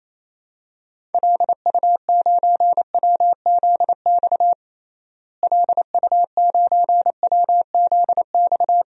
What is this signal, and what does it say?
Morse code, heard as a keyed tone, about 27 wpm, 700 Hz. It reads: LU9WZX LU9WZX
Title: DXCC-100 Call Signs - 6 Characters - Repeated 3x - 28wpm